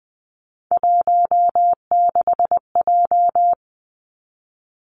Morse code, heard as a keyed tone, about 19 wpm, 700 Hz. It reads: 16J
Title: Three Challenging Letter-Number Combinations 20wpm